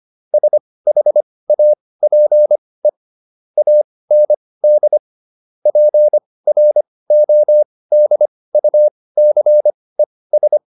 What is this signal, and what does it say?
Morse code, heard as a keyed tone, about 25 wpm, spaced as for 18 wpm, 600 Hz. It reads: SHAPE AND PRODUCES